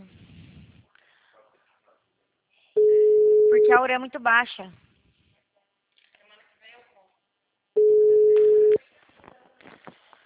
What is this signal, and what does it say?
Ringback tone (Europe, 425 Hz), with speech.